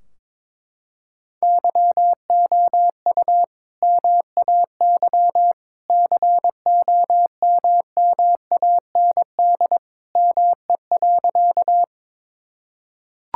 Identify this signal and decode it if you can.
Morse code, heard as a keyed tone, about 22 wpm, 700 Hz. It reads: YOU MAY COMMAND ME.
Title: Sentences from Top 1600 Words 22wpm